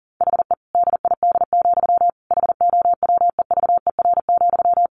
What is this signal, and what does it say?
Morse code, heard as a keyed tone, about 40 wpm, 700 Hz. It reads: HE DID, HOWEVER,
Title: Sentences from Top 1400 Words 40wpm